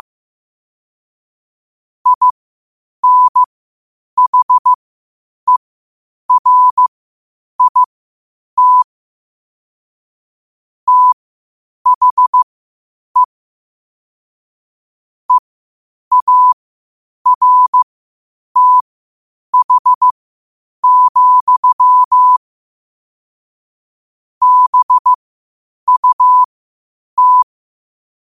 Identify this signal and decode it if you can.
Morse code, heard as a keyed tone, about 15 wpm, spaced as for 8 wpm, 1000 Hz. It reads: INHERIT THE EARTH, BUT
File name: qotdfiles / 8wpm